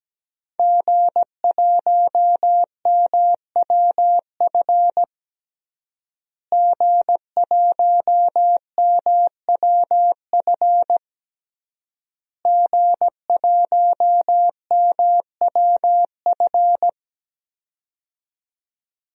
Morse code, heard as a keyed tone, about 17 wpm, 700 Hz. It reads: G1MWF G1MWF G1MWF
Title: DXCC-100 Call Signs - 5 Characters - Repeated 3x - 17wpm